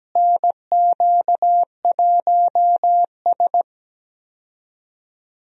Morse code, heard as a keyed tone, about 17 wpm, 700 Hz. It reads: NQ1S
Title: USA Call Signs - 4 Characters - Repeated 2x - 17wpm